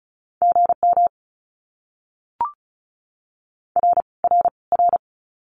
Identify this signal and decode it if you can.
Morse code, heard as a keyed tone, about 34 wpm, 700 Hz. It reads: GM R R R
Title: Abbreviations Repeated 3x - 35wpm